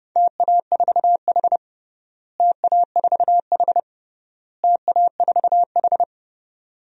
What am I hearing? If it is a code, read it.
Morse code, heard as a keyed tone, about 30 wpm, 700 Hz. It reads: TA4H TA4H TA4H
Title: DXCC-100 Call Signs - 4 Characters - Repeated 3x - 30wpm